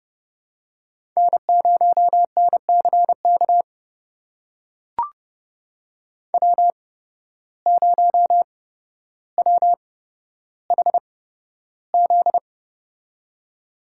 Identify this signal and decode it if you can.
Morse code, heard as a keyed tone, about 30 wpm, spaced as for 13 wpm, 700 Hz. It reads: N0NCK W0WHZ
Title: USA Call Signs - 5 Characters - 8x Character Spacing - 30wpm